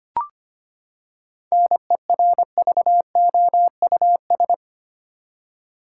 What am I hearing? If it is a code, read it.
Morse code, heard as a keyed tone, about 25 wpm, 700 Hz. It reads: NERVOUS
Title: Seven Letter Words - 25wpm